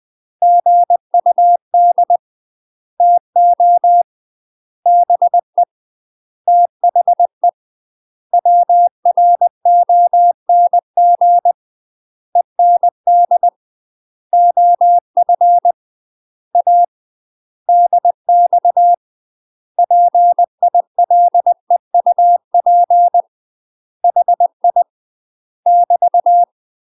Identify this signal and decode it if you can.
Morse code, heard as a keyed tone, about 20 wpm, 700 Hz. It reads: GUD TO BE THE WRONG END OF A DX PILEUP HI <BT>
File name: UK MASTER QSOs TRAINING / 20WPM / UK MASTER QSOs 0028